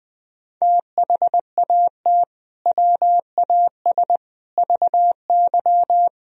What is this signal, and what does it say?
Morse code, heard as a keyed tone, about 20 wpm, 700 Hz. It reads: THAT WAS VY
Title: Sentences from Top 200 Words and Abbreviations 20wpm